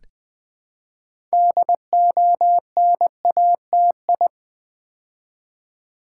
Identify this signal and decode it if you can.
Morse code, heard as a keyed tone, about 20 wpm, 700 Hz. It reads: DONATI
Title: Build A Word - 3 Letters At A Time - 8-Letter Words - 20wpm